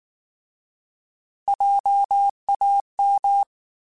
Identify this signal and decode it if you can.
Morse code, heard as a keyed tone, about 19 wpm, 800 Hz. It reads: JAM
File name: Alphabetical / Alphabetical mp3s / Js @18